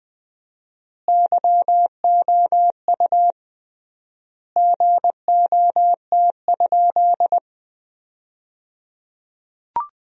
Morse code, heard as a keyed tone, about 20 wpm, 700 Hz. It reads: YOU GOT?
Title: 3x Word-Spacing - Sentences from Top 300 Words - 20wpm